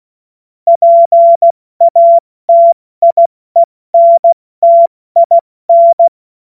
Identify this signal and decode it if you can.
Morse code, heard as a keyed tone, about 16 wpm, 680 Hz. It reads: PATIENTIN
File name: Morsekurs / Witze / Witz 27. Juni 2023 70 F80